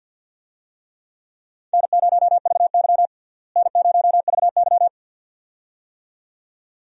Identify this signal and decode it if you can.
Morse code, heard as a keyed tone, about 49 wpm, 700 Hz. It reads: N0VY N0VY
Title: USA Call Signs - 4 Characters - Repeated 2x - 50wpm